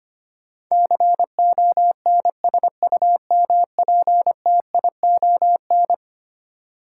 Morse code, heard as a keyed tone, about 25 wpm, 700 Hz. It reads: CONSUMPTION
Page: morse-code-ninja-practice-25wpm